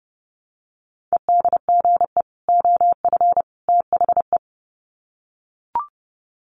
Morse code, heard as a keyed tone, about 30 wpm, 700 Hz. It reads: EDGE OF THE
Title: Sets of 3 Words 30wpm